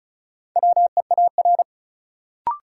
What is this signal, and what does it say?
Morse code, heard as a keyed tone, about 36 wpm, 700 Hz. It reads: WEAR
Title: Sentences from Top 1500 Words - New Word Drill Alt - 35wpm